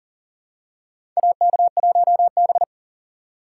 Morse code, heard as a keyed tone, about 40 wpm, 700 Hz. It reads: AK1B
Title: USA Call Signs - 4 Characters - 3x Character Spacing - 40wpm